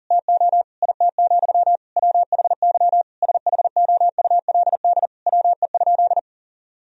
Morse code, heard as a keyed tone, about 37 wpm, 700 Hz. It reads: TO IT, WHY SHOULD WE?
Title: Sentences from Top 1000 Words - Review Part 3 - 40wpm